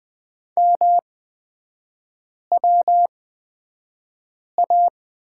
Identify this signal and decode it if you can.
Morse code, heard as a keyed tone, about 20 wpm, spaced as for 5 wpm, 700 Hz. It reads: MWA
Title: 236 Lesson 38e - Letter q - Three Characters 20wpm